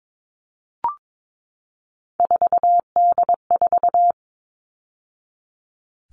Morse code, heard as a keyed tone, about 22 wpm, 700 Hz. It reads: 4D4
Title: All Cut Numbers - 3-Digit 22wpm